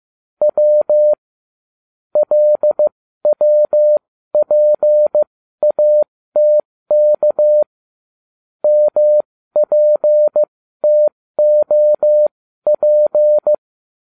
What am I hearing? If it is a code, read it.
Morse code, heard as a keyed tone, about 15 wpm, spaced as for 13 wpm, 600 Hz. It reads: W LWPATK MPTOP